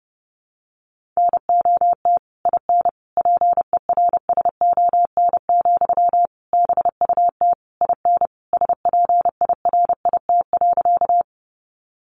Morse code, heard as a keyed tone, about 30 wpm, 700 Hz. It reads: NOT IN PERSON, BUT IN SPIRIT.
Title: Sentences from Top 1400 Words 30wpm